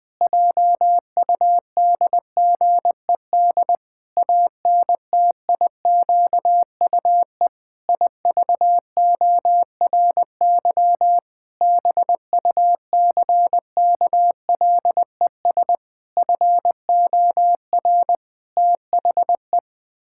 Morse code, heard as a keyed tone, about 19 wpm, 700 Hz. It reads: JUDGED ANTIQUE IVORY BUCKLES FOR THE
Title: Pangram Challenge v2 - 20wpm